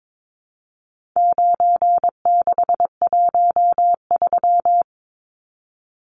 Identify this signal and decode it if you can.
Morse code, heard as a keyed tone, about 22 wpm, 700 Hz. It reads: 9613
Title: Numbers - 4-Digit 22wpm